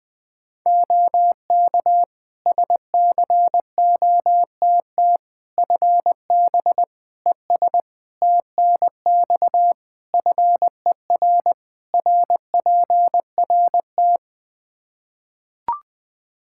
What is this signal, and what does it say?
Morse code, heard as a keyed tone, about 20 wpm, 700 Hz. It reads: OK SCOTT FB ES TNX FER RPRT
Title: Practice QSO - Exchange 2 - Info - 20wpm